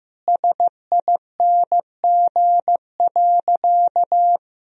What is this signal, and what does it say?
Morse code, heard as a keyed tone, about 15 wpm, 700 Hz. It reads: SING.
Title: Sentences from Top 1000 Words - Review Part 2 - 15wpm